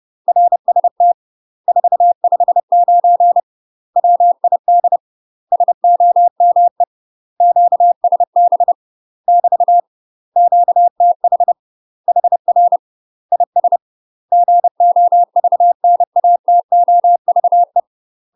Morse code, heard as a keyed tone, about 30 wpm, 700 Hz. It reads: RST 459 WID SOME QSB <BT> QTH HR IS GOVNATOVE
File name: UK MASTER QSOs TRAINING / 30WPM / UK MASTER QSOs 0042